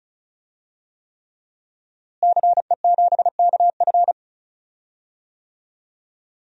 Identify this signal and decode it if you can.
Morse code, heard as a keyed tone, about 36 wpm, 700 Hz. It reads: CE7KF
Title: DXCC-100 Call Signs - 5 Characters - Repeated 3x - 35wpm